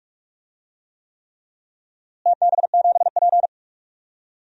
Morse code, heard as a keyed tone, about 42 wpm, 700 Hz. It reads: TB7P
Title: DXCC-100 Call Signs - 4 Characters - Repeated 3x - 45wpm